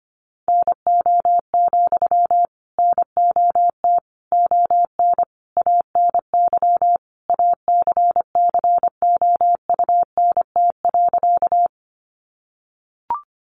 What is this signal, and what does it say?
Morse code, heard as a keyed tone, about 25 wpm, 700 Hz. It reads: NO, NOT ON ANY ACCOUNT.
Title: Sentences from Top 900 Words 25wpm